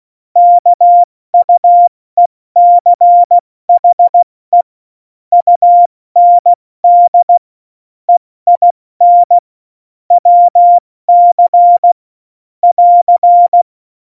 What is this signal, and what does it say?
Morse code, heard as a keyed tone, about 16 wpm, 700 Hz. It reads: KUECHE UND EIN WC <AR>